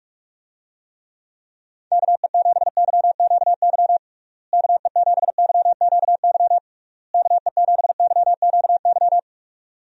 Morse code, heard as a keyed tone, about 46 wpm, 700 Hz. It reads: KE7YQY KE7YQY KE7YQY
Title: USA Call Signs - 6 Characters - Repeated 3x - 45wpm